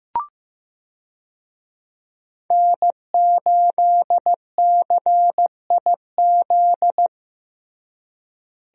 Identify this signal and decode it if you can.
Morse code, heard as a keyed tone, about 16 wpm, 700 Hz. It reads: N8CIZ
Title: USA Call Signs - 5 Characters - Repeated 2x - 15wpm